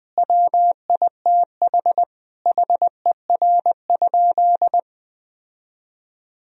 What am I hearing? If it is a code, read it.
Morse code, heard as a keyed tone, about 20 wpm, 700 Hz. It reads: WITH HER?
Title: Sentences from Top 1000 Words - Review Part 3 - 20wpm